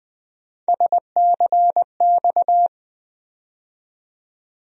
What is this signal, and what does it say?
Morse code, heard as a keyed tone, about 21 wpm, 700 Hz. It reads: SCX
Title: USA Call Sign Postfixes - 3 Letters 20wpm